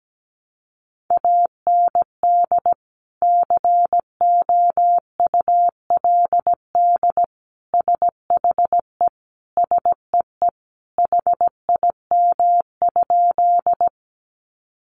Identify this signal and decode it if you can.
Morse code, heard as a keyed tone, about 17 wpm, 700 Hz. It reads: AND COULD SHE SEE HIM?
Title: Sentences from Top 100 Words 17wpm